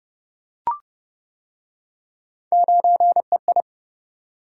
Morse code, heard as a keyed tone, about 27 wpm, 700 Hz. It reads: 9EI